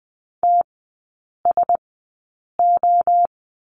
Morse code, heard as a keyed tone, about 20 wpm, spaced as for 8 wpm, 700 Hz. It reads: TSO
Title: 035 Lesson 7e - Letter s - Three Characters 20wpm